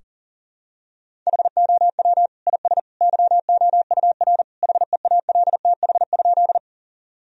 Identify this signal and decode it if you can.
Morse code, heard as a keyed tone, about 40 wpm, 700 Hz. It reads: HOW IS YOUR HEALTH?
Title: Sentences from Top 1000 Words - Review Part 3 - 40wpm